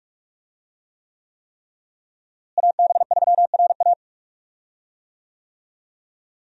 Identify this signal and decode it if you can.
Morse code, heard as a keyed tone, about 46 wpm, 700 Hz. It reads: AB3RA
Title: USA Call Signs - 5 Characters - Repeated 3x - 45wpm